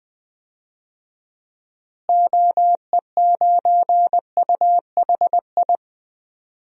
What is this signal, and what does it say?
Morse code, heard as a keyed tone, about 20 wpm, 700 Hz. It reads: OE9UHI
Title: DXCC-100 Call Signs - 6 Characters - Repeated 2x - 20wpm